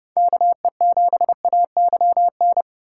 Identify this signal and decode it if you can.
Morse code, heard as a keyed tone, about 29 wpm, 700 Hz. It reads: KE7AYN